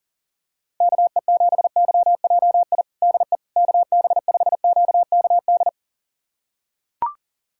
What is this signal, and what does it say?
Morse code, heard as a keyed tone, about 40 wpm, 700 Hz. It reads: KE7YJI DE KB5QKD
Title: USA Callsign DE Callsign - 6 Characters - 40wpm